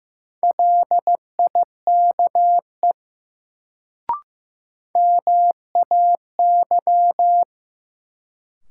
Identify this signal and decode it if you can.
Morse code, heard as a keyed tone, about 15 wpm, 700 Hz. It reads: LIKE MAY
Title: Sets of 2 Words - New Word Drill - 15wpm